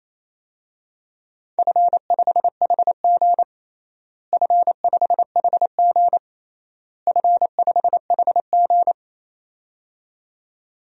Morse code, heard as a keyed tone, about 28 wpm, 700 Hz. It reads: F5HG F5HG F5HG
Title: DXCC-100 Call Signs - 4 Characters - Repeated 3x - 28wpm